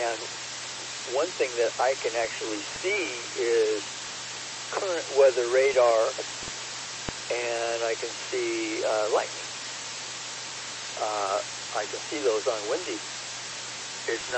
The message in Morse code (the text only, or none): none